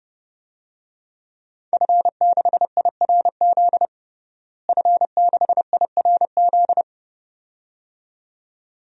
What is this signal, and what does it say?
Morse code, heard as a keyed tone, about 30 wpm, 700 Hz. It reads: F6IRZ F6IRZ
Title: DXCC-100 Call Signs - 5 Characters - Repeated 2x - 30wpm